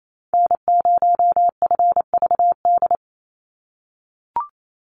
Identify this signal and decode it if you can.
Morse code, heard as a keyed tone, about 27 wpm, 700 Hz. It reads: N0FVD
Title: USA Call Signs 28wpm